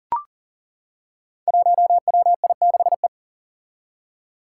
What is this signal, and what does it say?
Morse code, heard as a keyed tone, about 41 wpm, 700 Hz. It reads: 1WI6E
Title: Five Challenging Letter-Number Combinations 40wpm